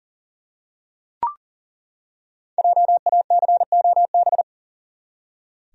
Morse code, heard as a keyed tone, about 40 wpm, 700 Hz. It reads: JACOB
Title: USA Common Names - 40wpm